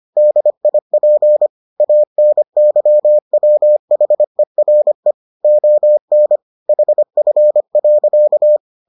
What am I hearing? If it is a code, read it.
Morse code, heard as a keyed tone, about 25 wpm, 600 Hz. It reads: DIP ANYWHERE ON HF.